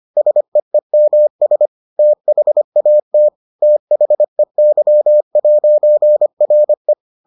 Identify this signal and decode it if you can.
Morse code, heard as a keyed tone, about 25 wpm, 600 Hz. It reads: SEEMS THAT THEY'RE